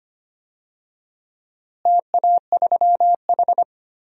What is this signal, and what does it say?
Morse code, heard as a keyed tone, about 25 wpm, 700 Hz. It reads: TA3H
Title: DXCC-100 Call Signs - 4 Characters - Repeated 3x - 25wpm